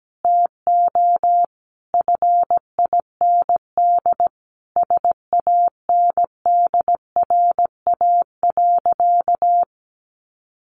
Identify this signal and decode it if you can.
Morse code, heard as a keyed tone, about 17 wpm, 700 Hz. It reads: TO FIND SANDRA.
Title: Easy Phrases 17wpm